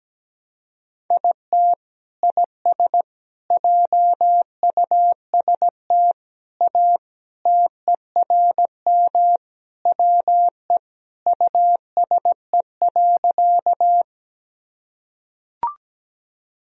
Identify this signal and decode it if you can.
Morse code, heard as a keyed tone, about 17 wpm, 700 Hz. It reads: IT IS JUST A TERM WE USE.
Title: Sentences from Top 1000 Words - Review Part 4 - 17wpm